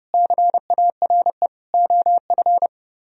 Morse code, heard as a keyed tone, about 39 wpm, 700 Hz. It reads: CARE OF